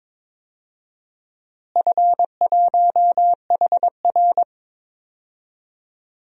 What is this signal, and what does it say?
Morse code, heard as a keyed tone, about 22 wpm, 700 Hz. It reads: F1HR
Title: DXCC-100 Call Signs - 4 Characters - Repeated 3x - 22wpm